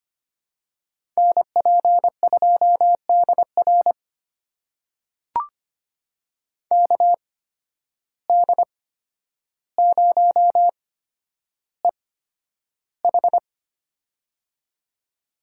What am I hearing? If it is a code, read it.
Morse code, heard as a keyed tone, about 25 wpm, spaced as for 11 wpm, 700 Hz. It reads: NP2DR KD0EH
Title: USA Call Signs - 5 Characters - 8x Character Spacing - 25wpm